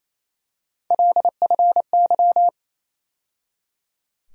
Morse code, heard as a keyed tone, about 28 wpm, 700 Hz. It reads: LFY
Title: USA Call Sign Postfixes - 3 Letters 28wpm